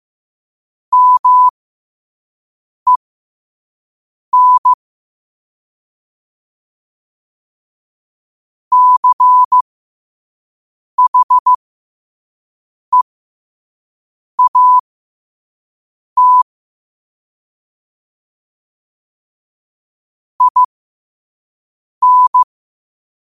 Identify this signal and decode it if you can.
Morse code, heard as a keyed tone, about 15 wpm, spaced as for 5 wpm, 1000 Hz. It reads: MEN CHEAT IN
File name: qotdfiles / 5wpm